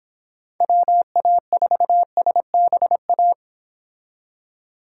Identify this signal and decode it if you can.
Morse code, heard as a keyed tone, about 25 wpm, 700 Hz. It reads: WA4SBA